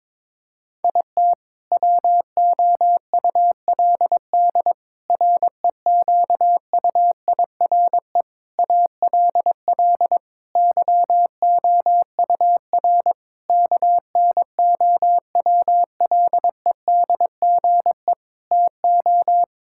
Morse code, heard as a keyed tone, about 22 wpm, 700 Hz. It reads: IT WOULD REQUIRE ALL YOUR KNOWLEDGE TO
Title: Sentences from Top 1000 Words - Review Part 6 - 22wpm